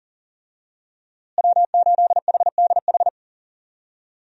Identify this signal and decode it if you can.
Morse code, heard as a keyed tone, about 40 wpm, 700 Hz. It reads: W8HDH
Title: USA Call Signs - 5 Characters - Repeated 2x - 40wpm